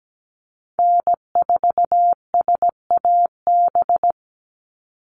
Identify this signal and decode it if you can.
Morse code, heard as a keyed tone, about 17 wpm, 700 Hz. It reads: N4SAB
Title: USA Call Signs 17wpm